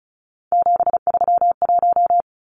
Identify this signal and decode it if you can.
Morse code, heard as a keyed tone, about 34 wpm, 700 Hz. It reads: 731
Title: Numbers - 3-Digit 35wpm